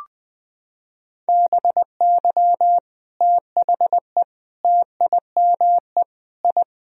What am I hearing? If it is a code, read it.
Morse code, heard as a keyed tone, about 20 wpm, 700 Hz. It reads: BY THE TIME I
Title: Sentences from Top 1000 Words - Review Part 1 - 20wpm